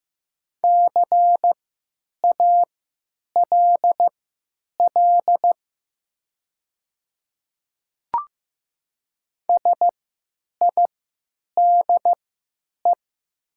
Morse code, heard as a keyed tone, about 15 wpm, spaced as for 9 wpm, 700 Hz. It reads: CALL SIDE
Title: 3x Character Spacing - Top 100 Words - 15wpm